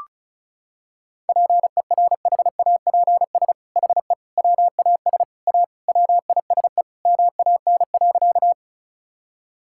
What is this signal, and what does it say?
Morse code, heard as a keyed tone, about 34 wpm, 700 Hz. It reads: PERHAPS HE WAS A WISE MAN.
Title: Sentences from Top 1600 Words 35wpm